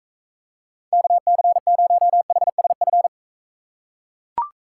Morse code, heard as a keyed tone, about 42 wpm, 700 Hz. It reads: KC0HSF